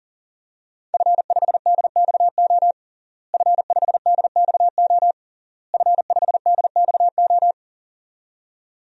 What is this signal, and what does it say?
Morse code, heard as a keyed tone, about 37 wpm, 700 Hz. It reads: F5DXO F5DXO F5DXO
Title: DXCC-100 Call Signs - 5 Characters - Repeated 3x - 40wpm